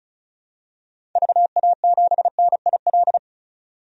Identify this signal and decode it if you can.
Morse code, heard as a keyed tone, about 34 wpm, 700 Hz. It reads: VA7NIL
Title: DXCC-100 Call Signs - 6 Characters - 35wpm